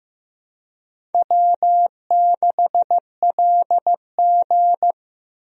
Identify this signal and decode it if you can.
Morse code, heard as a keyed tone, about 15 wpm, 700 Hz. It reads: W6LG
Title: USA Call Signs - 4 Characters - Repeated 3x - 15wpm